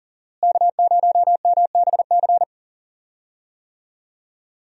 Morse code, heard as a keyed tone, about 39 wpm, 700 Hz. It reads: K0MBC
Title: USA Call Signs - 5 Characters - Repeated 3x - 40wpm